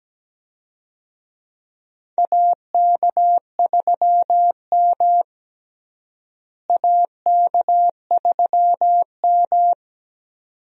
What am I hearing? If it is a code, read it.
Morse code, heard as a keyed tone, about 17 wpm, 700 Hz. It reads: AK3M AK3M
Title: USA Call Signs - 4 Characters - Repeated 2x - 17wpm